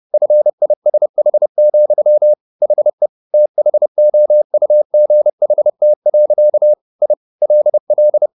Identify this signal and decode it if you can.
Morse code, heard as a keyed tone, about 30 wpm, 600 Hz. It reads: FISH, HE THOUGHT. I LL